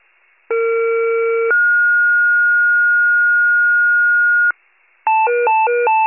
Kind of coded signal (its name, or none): two-tone page